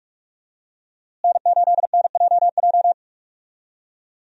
Morse code, heard as a keyed tone, about 46 wpm, 700 Hz. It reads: N8NJJ
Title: USA Call Signs - 5 Characters - Repeated 2x - 45wpm